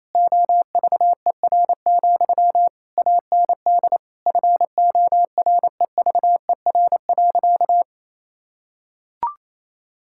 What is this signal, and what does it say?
Morse code, heard as a keyed tone, about 28 wpm, 700 Hz. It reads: OVER, AND FOREVER.
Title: Sentences from Top 1600 Words 28wpm